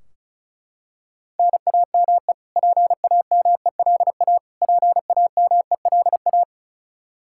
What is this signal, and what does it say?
Morse code, heard as a keyed tone, about 36 wpm, 700 Hz. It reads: NAME PAMELA PAMELA
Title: Practice QSO - Exchange 1 - Beginning - 35wpm